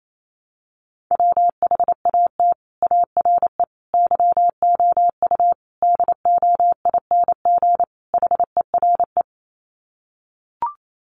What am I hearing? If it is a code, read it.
Morse code, heard as a keyed tone, about 28 wpm, 700 Hz. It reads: WHAT ARE YOU DOING HERE
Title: Sets of 5 Words 28wpm